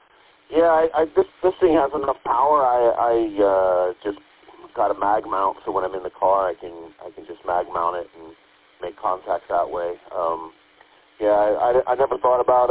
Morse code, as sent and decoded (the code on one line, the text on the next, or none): none